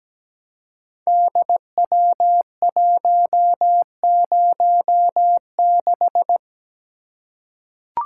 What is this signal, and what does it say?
Morse code, heard as a keyed tone, about 17 wpm, 700 Hz. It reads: DW106
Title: Code Groups - 5 Letter:Numbers - 17wpm